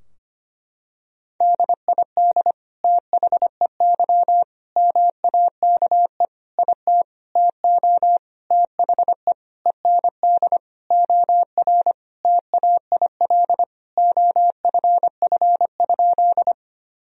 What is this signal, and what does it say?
Morse code, heard as a keyed tone, about 25 wpm, 700 Hz. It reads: DID THEY MAKE IT TO THE END OR TAIL OFF?
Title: Sentences from Top 1000 Words - Review Part 3 - 25wpm